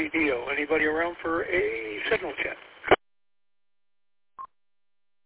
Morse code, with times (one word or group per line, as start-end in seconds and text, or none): none